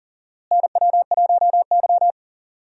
Morse code, heard as a keyed tone, about 40 wpm, 700 Hz. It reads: NW1Y
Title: USA Call Signs - 4 Characters - Repeated 2x - 40wpm